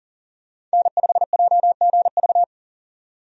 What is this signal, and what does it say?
Morse code, heard as a keyed tone, about 40 wpm, 700 Hz. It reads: N5JGV
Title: USA Call Signs - 5 Characters - 40wpm